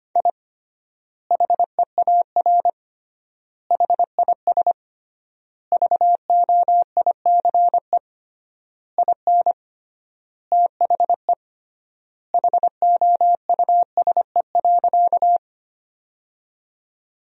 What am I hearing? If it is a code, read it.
Morse code, heard as a keyed tone, about 25 wpm, 700 Hz. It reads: I HEAR HIS VOICE IN THE HOUSE.
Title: 3x Word-Spacing - Sentences from Top 500 Words - 25wpm